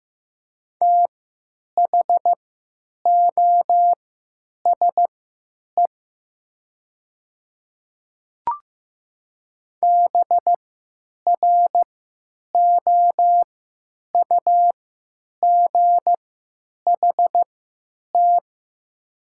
Morse code, heard as a keyed tone, about 15 wpm, spaced as for 9 wpm, 700 Hz. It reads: THOSE BROUGHT
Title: 3x Character Spacing - Top 500 Words - 15wpm